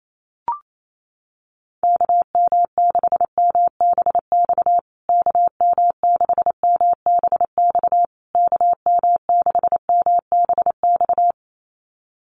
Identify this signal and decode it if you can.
Morse code, heard as a keyed tone, about 28 wpm, 700 Hz. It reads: KM6MBX KM6MBX KM6MBX
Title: USA Call Signs Repeated 3x 28wpm